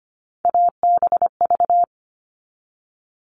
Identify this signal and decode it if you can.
Morse code, heard as a keyed tone, about 25 wpm, 700 Hz. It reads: ABV
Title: All Cut Numbers - 3-Digit 25wpm